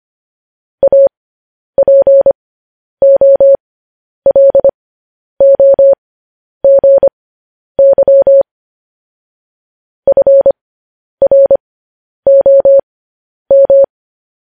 Morse code, heard as a keyed tone, about 25 wpm, spaced as for 10 wpm, 550 Hz. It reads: APOLOGY FROM
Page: news-headlines-in-morse-code-at-10-wpm